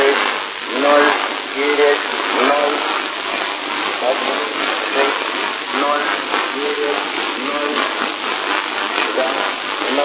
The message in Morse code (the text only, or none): none